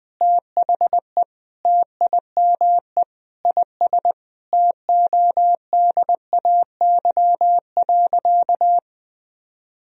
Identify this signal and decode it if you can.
Morse code, heard as a keyed tone, about 20 wpm, 700 Hz. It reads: THE TIME IS TODAY.
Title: Sentences from Top 1000 Words - Review Part 2 - 20wpm